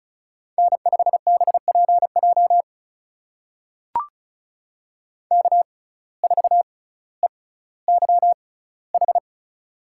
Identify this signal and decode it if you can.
Morse code, heard as a keyed tone, about 36 wpm, spaced as for 18 wpm, 700 Hz. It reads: N5BPJ K4EYH
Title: USA Call Signs - 5 Characters - 6x Character Spacing - 35wpm